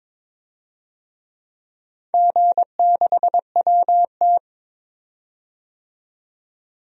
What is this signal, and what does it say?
Morse code, heard as a keyed tone, about 23 wpm, 700 Hz. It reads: G6WT
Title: DXCC-100 Call Signs - 4 Characters - 22wpm